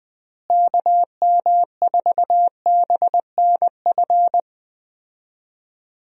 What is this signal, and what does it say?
Morse code, heard as a keyed tone, about 20 wpm, 700 Hz. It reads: KM4BNF
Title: USA Call Signs - 6 Characters - Repeated 3x - 20wpm